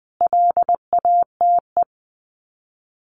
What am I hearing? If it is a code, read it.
Morse code, heard as a keyed tone, about 20 wpm, 700 Hz. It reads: LATE
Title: Sets of 5 Words Encore2 20wpm